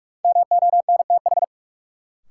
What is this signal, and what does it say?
Morse code, heard as a keyed tone, about 42 wpm, 700 Hz. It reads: MONTH
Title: Sets of 5 Words Encore - New Word Drill - 45wpm